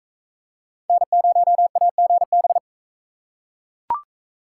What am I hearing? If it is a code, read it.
Morse code, heard as a keyed tone, about 39 wpm, 700 Hz. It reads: N0AGB